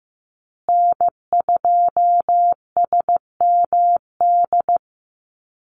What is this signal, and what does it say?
Morse code, heard as a keyed tone, about 15 wpm, 700 Hz. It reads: N2SMD
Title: USA Call Signs Repeated 2x 15wpm